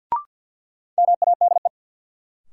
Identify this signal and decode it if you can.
Morse code, heard as a keyed tone, about 53 wpm, 700 Hz. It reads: MADE
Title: Sentences from Top 200 Words - New Word Drill - 50wpm